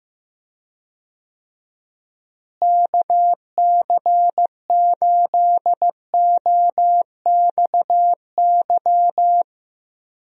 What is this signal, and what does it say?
Morse code, heard as a keyed tone, about 16 wpm, 700 Hz. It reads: KC8OXY
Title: USA Call Signs - 6 Characters - Repeated 3x - 15wpm